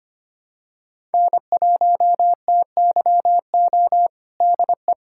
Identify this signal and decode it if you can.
Morse code, heard as a keyed tone, about 25 wpm, 700 Hz. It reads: N1TYO DE
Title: USA Callsign DE Callsign - 5 Characters - 25wpm